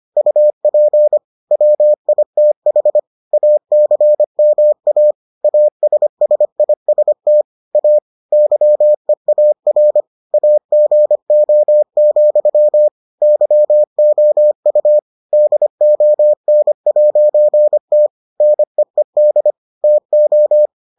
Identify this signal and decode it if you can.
Morse code, heard as a keyed tone, about 25 wpm, 600 Hz. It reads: UP WITH ACMA ASSIST A YEAR AGO, YOU DON'T NEED TO